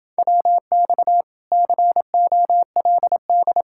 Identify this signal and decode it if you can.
Morse code, heard as a keyed tone, about 29 wpm, 700 Hz. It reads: WX COLD